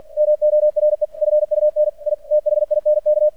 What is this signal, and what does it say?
Morse code, heard as a keyed tone, about 48 wpm, 600 Hz. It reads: MORE FUN I THINK
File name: archived / CRSnet